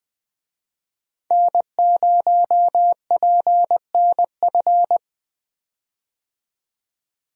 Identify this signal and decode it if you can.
Morse code, heard as a keyed tone, about 20 wpm, 700 Hz. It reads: N0PNF
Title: USA Call Signs - 5 Characters - 20wpm